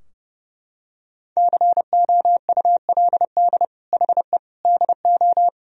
Morse code, heard as a keyed tone, about 30 wpm, 700 Hz. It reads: COULD HE DO
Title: Sentences from Top 1000 Words - Review Part 2 - 30wpm